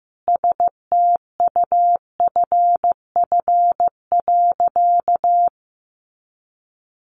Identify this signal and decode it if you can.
Morse code, heard as a keyed tone, about 15 wpm, 700 Hz. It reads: STUFF.
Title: Sentences from Top 1400 Words 15wpm